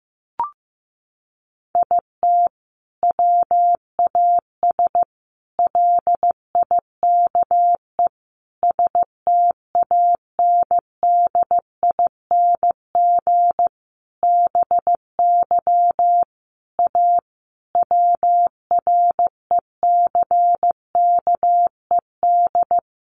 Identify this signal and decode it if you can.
Morse code, heard as a keyed tone, about 15 wpm, 700 Hz. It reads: IT WAS LIKE STANDING BY A WRECKED
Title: A Sound Of Thunder 15wpm